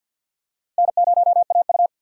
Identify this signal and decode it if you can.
Morse code, heard as a keyed tone, about 45 wpm, 700 Hz. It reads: N0AU
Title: USA Call Signs - 4 Characters - Repeated 3x - 50wpm